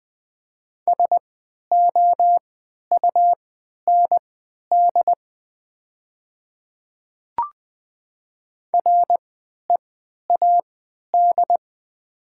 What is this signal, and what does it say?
Morse code, heard as a keyed tone, about 20 wpm, spaced as for 11 wpm, 700 Hz. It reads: SOUND READ
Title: 3x Character Spacing - Top 200 Words - 20wpm